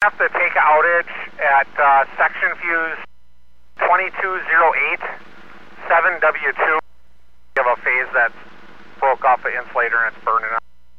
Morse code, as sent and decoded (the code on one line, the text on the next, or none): none